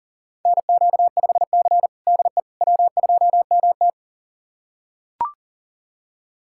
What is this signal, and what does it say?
Morse code, heard as a keyed tone, about 40 wpm, 700 Hz. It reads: NQ5C DE W2MT
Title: USA Callsign DE Callsign - 4 Characters - 40wpm